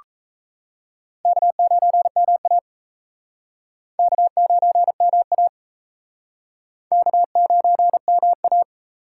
Morse code, heard as a keyed tone, about 36 wpm, 700 Hz. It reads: K9MA K9MA K9MA